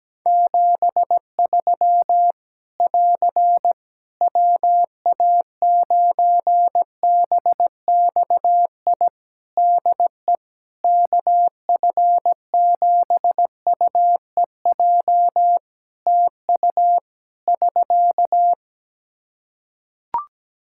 Morse code, heard as a keyed tone, about 17 wpm, 700 Hz. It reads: 73 <AR> WA9BXI DE KF7UEJ TU <SK>
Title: K1USN SST QSO Practice 17wpm